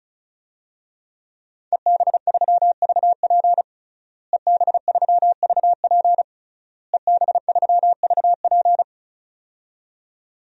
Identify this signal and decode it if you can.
Morse code, heard as a keyed tone, about 36 wpm, 700 Hz. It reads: EB3VP EB3VP EB3VP
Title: DXCC-100 Call Signs - 5 Characters - Repeated 3x - 35wpm